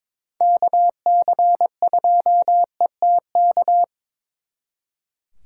Morse code, heard as a keyed tone, about 26 wpm, 700 Hz. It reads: KC2ETK
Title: Speed Racing - USA Call Signs 22wpm